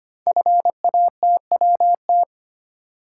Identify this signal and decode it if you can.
Morse code, heard as a keyed tone, about 25 wpm, 700 Hz. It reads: FATWT
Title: Code Groups - 5 Letters - 25wpm